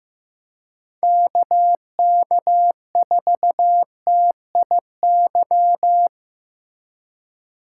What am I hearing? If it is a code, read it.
Morse code, heard as a keyed tone, about 15 wpm, 700 Hz. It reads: KK4TIY
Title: USA Call Signs - 6 Characters - 3x Character Spacing - 15wpm